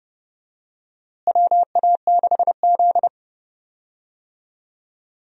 Morse code, heard as a keyed tone, about 32 wpm, 700 Hz. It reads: WA6Z
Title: USA Call Signs - 4 Characters - Repeated 3x - 30wpm